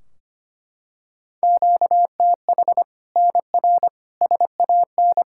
Morse code, heard as a keyed tone, about 25 wpm, 700 Hz. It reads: QTH NR SAN
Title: Practice QSO - Exchange 1 - Beginning - 25wpm